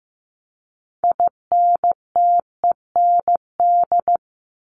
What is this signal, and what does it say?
Morse code, heard as a keyed tone, about 15 wpm, 700 Hz. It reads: INTEND
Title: Top 1501-1750 Words 15wpm